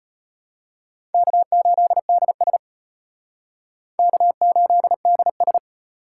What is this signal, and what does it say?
Morse code, heard as a keyed tone, about 31 wpm, 700 Hz. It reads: K8DS K8DS